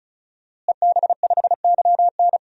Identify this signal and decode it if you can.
Morse code, heard as a keyed tone, about 34 wpm, 700 Hz. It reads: EB5YN
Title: DXCC-100 Call Signs - 5 Characters - Repeated 2x - 35wpm